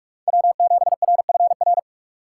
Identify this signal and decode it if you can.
Morse code, heard as a keyed tone, about 46 wpm, 700 Hz. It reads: W7RFR
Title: USA Call Signs - 5 Characters - 4x Character Spacing - 45wpm